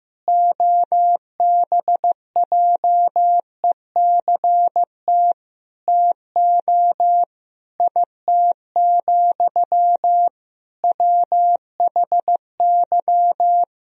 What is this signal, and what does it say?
Morse code, heard as a keyed tone, about 15 wpm, 700 Hz. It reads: OBJECT TO IT, WHY
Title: Sentences from Top 1000 Words - Review Part 3 - 15wpm